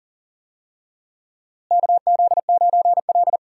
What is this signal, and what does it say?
Morse code, heard as a keyed tone, about 40 wpm, 700 Hz. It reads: KZ9L
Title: USA Call Signs - 4 Characters - Repeated 3x - 40wpm